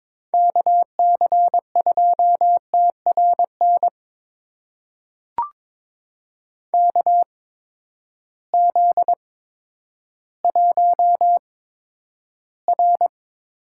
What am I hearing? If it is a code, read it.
Morse code, heard as a keyed tone, about 22 wpm, spaced as for 11 wpm, 700 Hz. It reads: KC2TRN KZ1R
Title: USA Call Signs - 8x Character Spacing - 22wpm